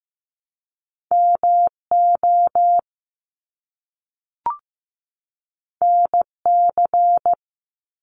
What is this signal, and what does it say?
Morse code, heard as a keyed tone, about 15 wpm, 700 Hz. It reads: MO NC
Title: ARRL Sections 15wpm